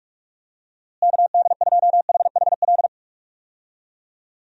Morse code, heard as a keyed tone, about 42 wpm, 700 Hz. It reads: KD2HHL
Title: USA Call Signs - 6 Characters - Repeated 3x - 45wpm